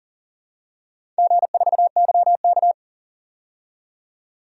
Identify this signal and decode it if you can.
Morse code, heard as a keyed tone, about 38 wpm, 700 Hz. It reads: G4YK
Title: DXCC-100 Call Signs - 4 Characters - 40wpm